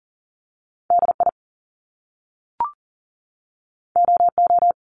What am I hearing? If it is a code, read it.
Morse code, heard as a keyed tone, about 38 wpm, 700 Hz. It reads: DI OO
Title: Top Two Letter Combinations in Words 40wpm